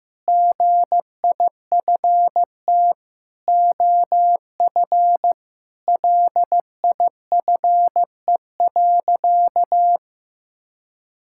Morse code, heard as a keyed tone, about 15 wpm, 700 Hz. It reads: GIFT OF LIFE.
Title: Sentences from Top 1000 Words - Review Part 6 - 15wpm